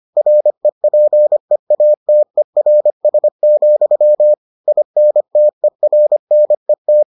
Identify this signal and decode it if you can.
Morse code, heard as a keyed tone, about 25 wpm, 600 Hz. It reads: REPEATERS, INTERNET